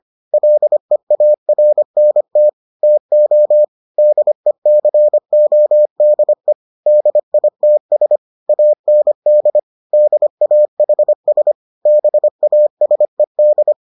Morse code, heard as a keyed tone, about 25 wpm, 600 Hz. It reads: LEARNT TO DECODE DITS AND DAHS BASED